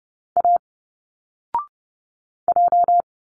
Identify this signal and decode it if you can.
Morse code, heard as a keyed tone, about 30 wpm, 700 Hz. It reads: A J
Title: Single Letters 30wpm